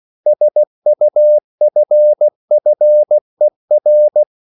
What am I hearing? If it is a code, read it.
Morse code, heard as a keyed tone, about 16 wpm, 600 Hz. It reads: SUFFER
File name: CW / Practice / EnglishQuotations0116